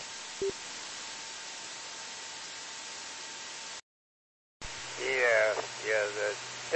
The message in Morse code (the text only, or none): none